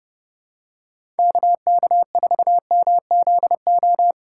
Morse code, heard as a keyed tone, about 30 wpm, 700 Hz. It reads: KK4MZO
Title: USA Call Signs - 6 Characters - 2x Character Spacing - 30wpm